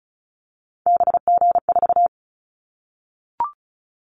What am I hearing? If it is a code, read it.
Morse code, heard as a keyed tone, about 36 wpm, 700 Hz. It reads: BG4
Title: Callsign Prefixes for Top 100 DXCC - 35wpm